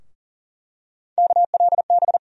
Morse code, heard as a keyed tone, about 40 wpm, 700 Hz. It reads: KLB
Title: USA Call Sign Postfixes - 3 Letters 40wpm